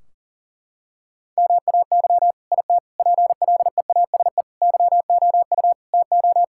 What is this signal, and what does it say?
Morse code, heard as a keyed tone, about 40 wpm, 700 Hz. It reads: MAY IT PLEASE YOU TO
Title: Sentences from Top 1600 Words 40wpm